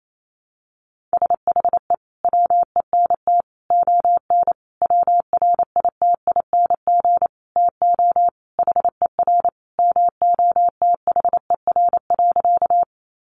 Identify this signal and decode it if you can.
Morse code, heard as a keyed tone, about 28 wpm, 700 Hz. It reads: SHE WENT ON WRITING TO HER MOTHER.
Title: Sentences from Top 800 Words 28wpm